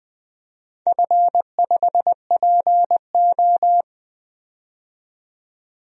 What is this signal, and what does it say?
Morse code, heard as a keyed tone, about 20 wpm, 700 Hz. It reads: F5PO
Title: DXCC-100 Call Signs - 4 Characters - 20wpm